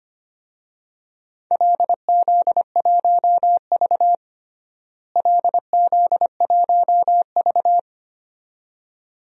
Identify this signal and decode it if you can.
Morse code, heard as a keyed tone, about 25 wpm, 700 Hz. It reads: LZ1V LZ1V
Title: DXCC-100 Call Signs - 4 Characters - Repeated 2x - 25wpm